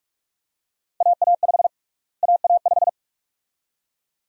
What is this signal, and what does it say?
Morse code, heard as a keyed tone, about 46 wpm, 700 Hz. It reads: AA5 AA5
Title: USA Call Sign Prefixes - Repeated 2x - 45wpm